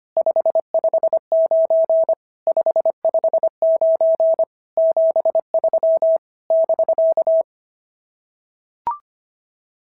Morse code, heard as a keyed tone, about 25 wpm, 690 Hz. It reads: 559 559 73 <BK>
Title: SOTA QSO Practice 25wpm